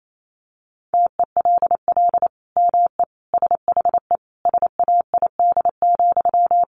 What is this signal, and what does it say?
Morse code, heard as a keyed tone, about 28 wpm, 700 Hz. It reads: TELL ME SHE SAID,
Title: Sentences from Top 1200 Words 28wpm